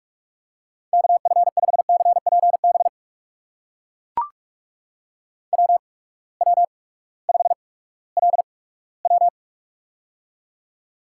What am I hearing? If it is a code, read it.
Morse code, heard as a keyed tone, about 44 wpm, spaced as for 20 wpm, 700 Hz. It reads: KF5CPB WW5LW
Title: USA Call Signs - 8x Character Spacing - 45wpm